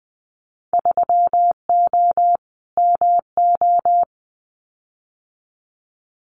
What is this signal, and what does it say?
Morse code, heard as a keyed tone, about 20 wpm, 700 Hz. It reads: 3O MO
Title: Winter Field Day Exchanges Encore 20wpm